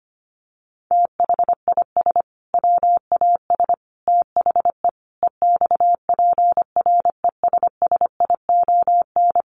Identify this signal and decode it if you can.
Morse code, heard as a keyed tone, about 25 wpm, 700 Hz. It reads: THIS WAS THE EXPRESSION
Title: Sentences from Top 1200 Words 25wpm